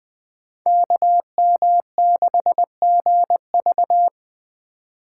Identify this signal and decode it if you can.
Morse code, heard as a keyed tone, about 20 wpm, 700 Hz. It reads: KM6GV
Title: USA Call Signs - 5 Characters - 3x Character Spacing - 20wpm